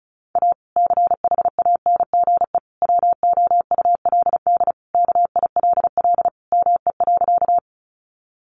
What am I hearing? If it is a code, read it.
Morse code, heard as a keyed tone, about 34 wpm, 700 Hz. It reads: A CHANGE WOULD KILL ME.
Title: Sentences from Top 1400 Words 35wpm